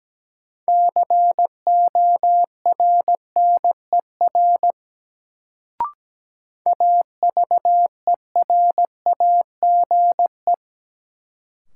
Morse code, heard as a keyed tone, about 17 wpm, 700 Hz. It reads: CORNER AVERAGE
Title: Sets of 5 Words - New Word Drill - 17wpm